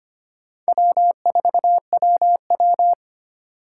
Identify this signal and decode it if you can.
Morse code, heard as a keyed tone, about 25 wpm, 700 Hz. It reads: W4WW
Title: USA Call Signs - 4 Characters - 4x Character Spacing - 25wpm